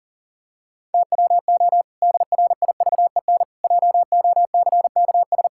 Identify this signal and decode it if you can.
Morse code, heard as a keyed tone, about 40 wpm, 700 Hz. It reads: TWO DRIVEN JOCKS
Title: Pangram Challenge v2 - 40wpm